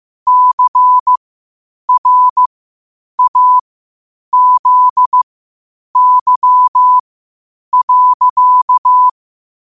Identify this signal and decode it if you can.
Morse code, heard as a keyed tone, about 15 wpm, spaced as for 8 wpm, 1000 Hz. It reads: CRAZY.
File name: qotdfiles / 8wpm